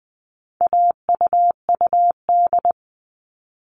Morse code, heard as a keyed tone, about 20 wpm, 700 Hz. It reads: AUUD
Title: All Cut Numbers - 4-Digit 20wpm